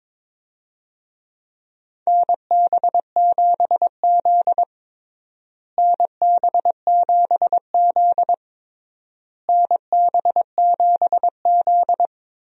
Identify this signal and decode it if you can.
Morse code, heard as a keyed tone, about 22 wpm, 700 Hz. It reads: NB7Z NB7Z NB7Z
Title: USA Call Signs - 4 Characters - Repeated 3x - 22wpm